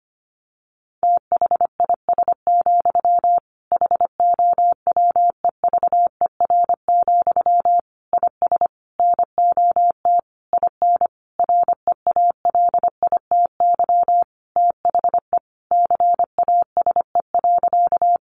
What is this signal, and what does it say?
Morse code, heard as a keyed tone, about 25 wpm, 700 Hz. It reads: THIS, HOWEVER, IS NOT IN REALITY THE CASE.
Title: Sentences from Top 1200 Words 25wpm